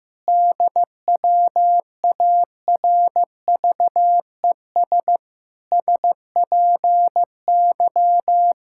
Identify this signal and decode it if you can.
Morse code, heard as a keyed tone, about 15 wpm, 700 Hz. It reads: DWARVES SPY
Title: Pangram Challenge v2 - 15wpm